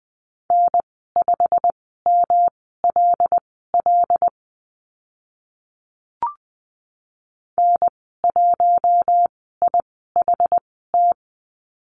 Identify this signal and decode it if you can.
Morse code, heard as a keyed tone, about 20 wpm, spaced as for 14 wpm, 700 Hz. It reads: N5MLL N1IHT
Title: 122 Lesson 20g - Number 3 - Call Signs 20wpm